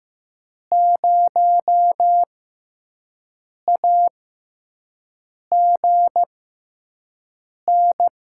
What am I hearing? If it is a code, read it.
Morse code, heard as a keyed tone, about 15 wpm, spaced as for 5 wpm, 700 Hz. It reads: 0AGN
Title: USA Call Signs - 6 Characters - 6x Character Spacing - 15wpm